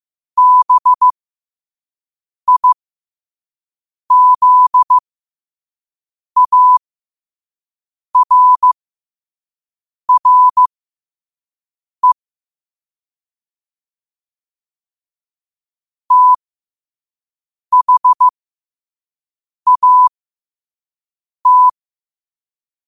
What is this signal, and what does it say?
Morse code, heard as a keyed tone, about 15 wpm, spaced as for 5 wpm, 1000 Hz. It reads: BIZARRE THAT
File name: qotdfiles / 5wpm